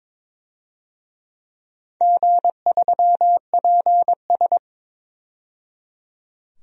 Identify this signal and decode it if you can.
Morse code, heard as a keyed tone, about 22 wpm, 700 Hz. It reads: G3PS
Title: DXCC-100 Call Signs - 4 Characters - 22wpm